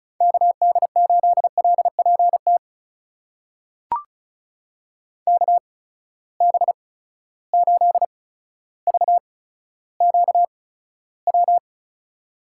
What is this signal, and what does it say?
Morse code, heard as a keyed tone, about 36 wpm, 700 Hz. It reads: K D 8 L P T K B 8 V Q W
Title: USA Call Signs - 8x Character Spacing - 35wpm